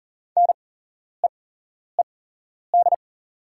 Morse code, heard as a keyed tone, about 42 wpm, spaced as for 11 wpm, 700 Hz. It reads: NEED
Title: 8x Character Spacing - Top 200 Words - 40wpm